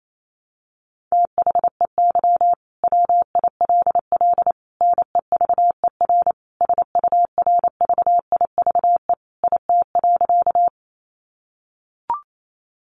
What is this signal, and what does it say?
Morse code, heard as a keyed tone, about 28 wpm, 700 Hz. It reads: THEY WILL NEVER SURVIVE IT.
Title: Sentences from Top 1300 Words 28wpm